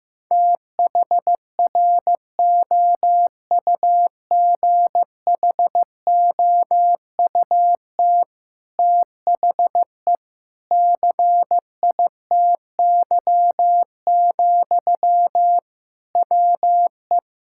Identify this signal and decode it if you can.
Morse code, heard as a keyed tone, about 15 wpm, 700 Hz. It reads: THROUGHOUT THE CITY, WE
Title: Sentences from Top 1000 Words - Review Part 5 - 15wpm